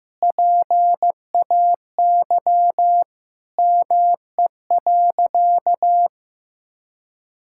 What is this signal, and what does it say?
Morse code, heard as a keyed tone, about 15 wpm, 700 Hz. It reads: PAY ME.
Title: Sentences from Top 1000 Words - Review Part 5 - 15wpm